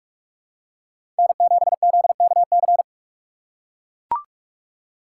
Morse code, heard as a keyed tone, about 42 wpm, 700 Hz. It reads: N7ZKC
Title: USA Call Signs - 6x Character Spacing - 45wpm